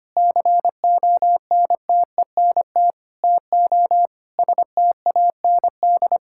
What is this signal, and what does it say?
Morse code, heard as a keyed tone, about 25 wpm, 700 Hz. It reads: CONTENT TO STAND
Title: Sentences from Top 1000 Words - Review Part 6 - 25wpm